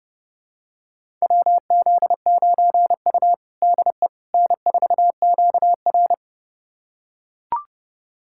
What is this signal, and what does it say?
Morse code, heard as a keyed tone, about 32 wpm, 700 Hz. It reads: WZ9U DE N4QR
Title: USA Callsign DE Callsign - 4 Characters - 30wpm